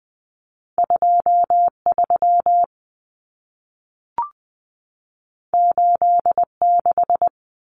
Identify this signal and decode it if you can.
Morse code, heard as a keyed tone, about 20 wpm, 700 Hz. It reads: 23 86
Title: Common Cut Numbers - 2-Digit 20wpm